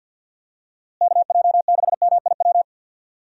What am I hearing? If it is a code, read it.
Morse code, heard as a keyed tone, about 51 wpm, 700 Hz. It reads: KJ6MIW
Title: USA Call Signs - 6 Characters - Repeated 2x - 50wpm